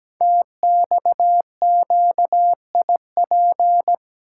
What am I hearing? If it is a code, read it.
Morse code, heard as a keyed tone, about 17 wpm, 700 Hz. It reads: TXQIP
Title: Code Groups - 5 Letters - 17wpm